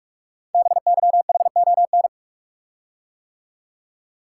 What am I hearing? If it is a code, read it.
Morse code, heard as a keyed tone, about 44 wpm, 690 Hz. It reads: BYHON
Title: Code Groups - 5 Letters - 45wpm